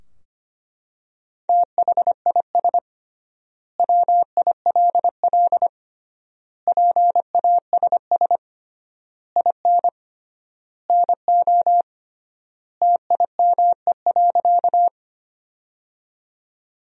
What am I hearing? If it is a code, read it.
Morse code, heard as a keyed tone, about 25 wpm, 700 Hz. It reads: THIS WILL PASS IN NO TIME.
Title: 3x Word-Spacing - Sentences from Top 400 Words - 25wpm